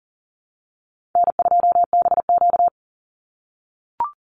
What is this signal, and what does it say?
Morse code, heard as a keyed tone, about 40 wpm, 690 Hz. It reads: N2BQ
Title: USA Call Signs 40wpm